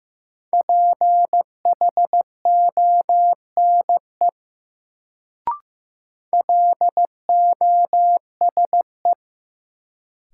Morse code, heard as a keyed tone, about 15 wpm, 700 Hz. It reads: PHONE LOSE
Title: Sentences from Top 1100 Words - New Word Drill Alt - 15wpm